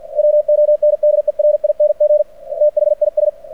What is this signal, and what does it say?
Morse code, heard as a keyed tone, about 49 wpm, 600 Hz. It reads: WONDERING THIS
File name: archived / CRSnet